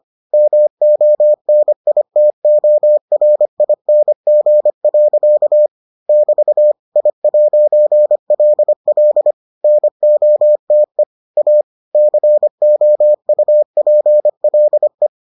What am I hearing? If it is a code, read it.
Morse code, heard as a keyed tone, about 25 wpm, 600 Hz. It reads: MONITORING. <BT> I'LL NOTE A COUPLE